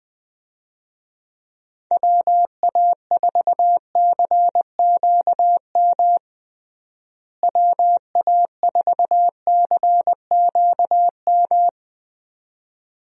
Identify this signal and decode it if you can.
Morse code, heard as a keyed tone, about 21 wpm, 700 Hz. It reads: WA4CQM WA4CQM
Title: USA Call Signs - 6 Characters - Repeated 2x - 20wpm